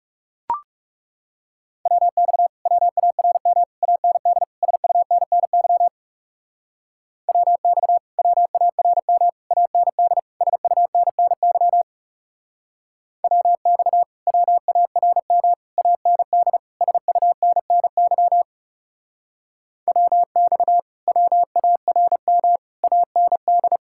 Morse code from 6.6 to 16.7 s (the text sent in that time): WX WARM AND SUNNY WX WARM AND